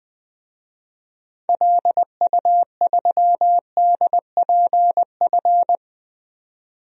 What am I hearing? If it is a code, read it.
Morse code, heard as a keyed tone, about 21 wpm, 700 Hz. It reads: LU3DPF
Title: DXCC-100 Call Signs - 6 Characters - 20wpm